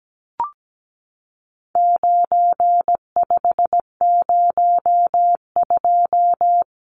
Morse code, heard as a keyed tone, about 17 wpm, 700 Hz. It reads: 9502
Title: Numbers - 4-Digit 17wpm